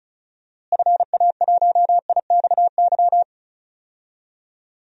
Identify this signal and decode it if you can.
Morse code, heard as a keyed tone, about 36 wpm, 700 Hz. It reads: FA1IXY
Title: DXCC-100 Call Signs - 6 Characters - Repeated 2x - 35wpm